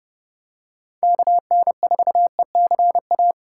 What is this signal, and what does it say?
Morse code, heard as a keyed tone, about 30 wpm, 700 Hz. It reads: KN4ECA